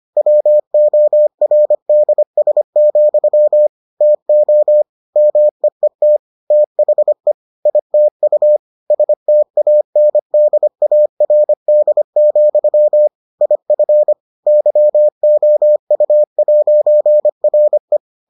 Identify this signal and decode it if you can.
Morse code, heard as a keyed tone, about 25 wpm, 600 Hz. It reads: WORDS, TO MEET THE ITU STANDARD, IF YOU'RE